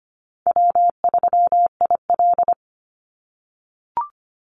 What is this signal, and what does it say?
Morse code, heard as a keyed tone, about 25 wpm, 700 Hz. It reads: W3IL
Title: Easy Call Signs - 25wpm